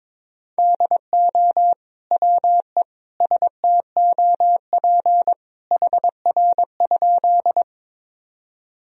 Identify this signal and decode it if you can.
Morse code, heard as a keyed tone, about 22 wpm, 700 Hz. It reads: DO WE STOP HR?
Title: Sentences from Top 300 Words and Abbreviations 22wpm